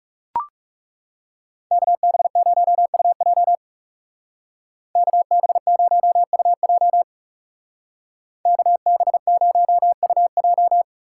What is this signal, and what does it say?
Morse code, heard as a keyed tone, about 37 wpm, 700 Hz. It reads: KB0UJ KB0UJ KB0UJ
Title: Speed Racing - USA Call Signs 30wpm